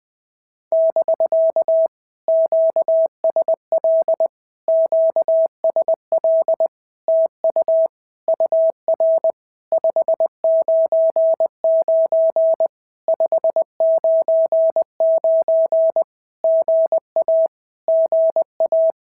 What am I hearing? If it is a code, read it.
Morse code, heard as a keyed tone, about 20 wpm, 650 Hz. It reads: <BK> QSL QSL TU UR 599 599 GA GA
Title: POTA QSO Practice - 20wpm